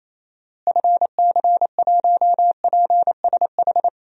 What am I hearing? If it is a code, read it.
Morse code, heard as a keyed tone, about 27 wpm, 700 Hz. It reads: FC1PSH
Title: DXCC-100 Call Signs - 6 Characters - Repeated 2x - 28wpm